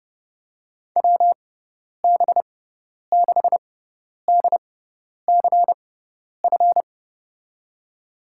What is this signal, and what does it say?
Morse code, heard as a keyed tone, about 30 wpm, spaced as for 10 wpm, 700 Hz. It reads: WB6DCF
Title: USA Call Signs - 6 Characters - 6x Character Spacing - 30wpm